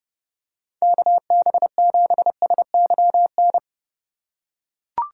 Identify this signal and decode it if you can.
Morse code, heard as a keyed tone, about 30 wpm, 700 Hz. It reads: KB7SYN
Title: USA Call Signs - 6 Characters - 6x Character Spacing - 30wpm